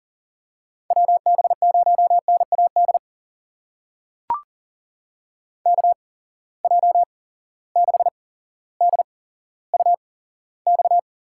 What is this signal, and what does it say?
Morse code, heard as a keyed tone, about 40 wpm, spaced as for 17 wpm, 700 Hz. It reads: WB0NAD KJ6DUX
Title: USA Call Signs - 6 Characters - 8x Character Spacing - 40wpm